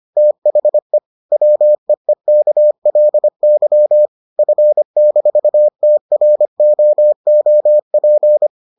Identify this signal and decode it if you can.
Morse code, heard as a keyed tone, about 25 wpm, 600 Hz. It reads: THE WEEKLY F-TROOP